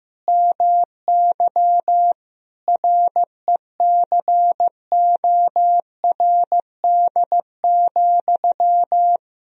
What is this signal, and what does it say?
Morse code, heard as a keyed tone, about 16 wpm, 700 Hz. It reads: MY RECORD,
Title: Sentences from Top 1000 Words - Review Part 3 - 15wpm